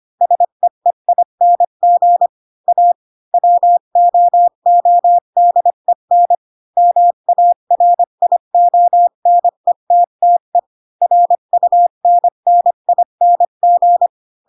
Morse code, heard as a keyed tone, about 25 wpm, 700 Hz. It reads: SEEING A WOODEN MARIONETTE RUNNING
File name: Pinocchio / Pinocchio ch3 20 25 ik1hge